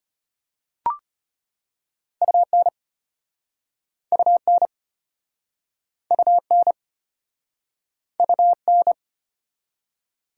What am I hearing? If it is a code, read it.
Morse code, heard as a keyed tone, about 31 wpm, 690 Hz. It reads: UN UN UN UN
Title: Speed-Racing Prefixes and Suffixes 25wpm